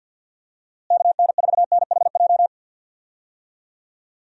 Morse code, heard as a keyed tone, about 51 wpm, 700 Hz. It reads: KN4NHJ
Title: USA Call Signs - 6 Characters - Repeated 2x - 50wpm